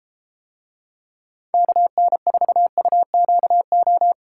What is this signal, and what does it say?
Morse code, heard as a keyed tone, about 33 wpm, 700 Hz. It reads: KN4UQO